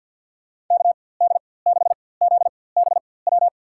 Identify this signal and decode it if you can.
Morse code, heard as a keyed tone, about 52 wpm, spaced as for 23 wpm, 700 Hz. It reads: KD6ZBW
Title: USA Call Signs - 6 Characters - 4x Character Spacing - 50wpm